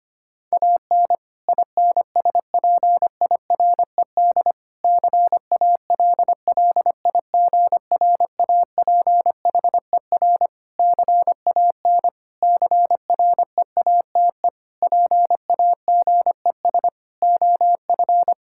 Morse code, heard as a keyed tone, about 25 wpm, 700 Hz. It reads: AN INSPIRED CALLIGRAPHER CAN CREATE PAGES OF
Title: Pangram Challenge v2 - 25wpm